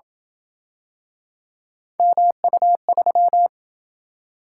Morse code, heard as a keyed tone, about 27 wpm, 700 Hz. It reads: MU3